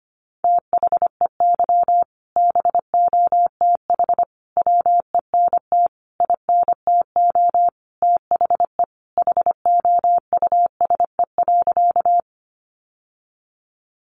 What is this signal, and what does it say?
Morse code, heard as a keyed tone, about 25 wpm, 700 Hz. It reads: THEY BOTH WENT INTO THE HOUSE.
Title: Sentences from Top 600 Words 25wpm